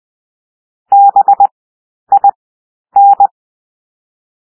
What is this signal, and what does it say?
Morse code, heard as a keyed tone, about 20 wpm, 800 Hz. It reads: B I N
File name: cwmp3 / W1 SOUND WORDS 20 wpm